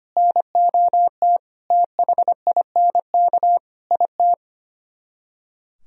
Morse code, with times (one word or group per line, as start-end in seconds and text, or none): 0.2-1.4 NOT
1.7-3.6 THINK
3.9-4.3 IT